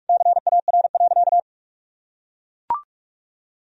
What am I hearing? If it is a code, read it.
Morse code, heard as a keyed tone, about 41 wpm, 700 Hz. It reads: CAR.
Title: Sentences from Top 1000 Words - Review Part 1 - 45wpm